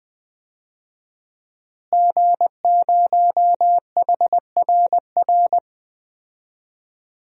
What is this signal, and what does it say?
Morse code, heard as a keyed tone, about 20 wpm, 700 Hz. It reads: G0HRR
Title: DXCC-100 Call Signs - 5 Characters - Repeated 3x - 20wpm